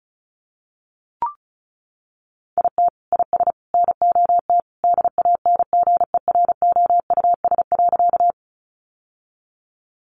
Morse code, heard as a keyed tone, about 34 wpm, 700 Hz. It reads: IT IS NOT DANGEROUS.
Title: Sentences from Top 1200 Words 35wpm